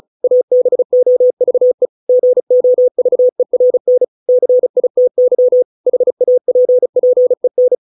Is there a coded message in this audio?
Yes: Morse code from 0.2 to 7.8 s, heard as a keyed tone, about 34 wpm, 500 Hz. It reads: ABOVE GOVERN CITY HAPPEN